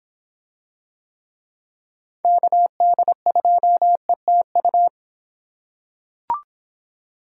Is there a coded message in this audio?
Yes: Morse code from 2.2 to 4.9 s, heard as a keyed tone, about 24 wpm, 700 Hz. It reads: KD2ETU